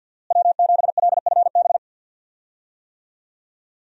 Morse code, heard as a keyed tone, about 46 wpm, 700 Hz. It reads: W7LFB
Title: USA Call Signs - 5 Characters - Repeated 3x - 50wpm